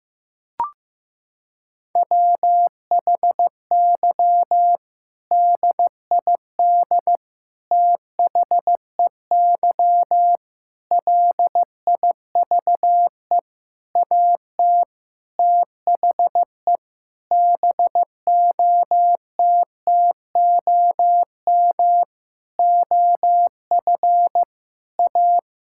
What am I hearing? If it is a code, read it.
Morse code, heard as a keyed tone, about 16 wpm, 700 Hz. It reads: WHY DID THEY LIVE AT THE BOTTOM OF A
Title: Sentences from Top 1000 Words - Review Part 6 - 15wpm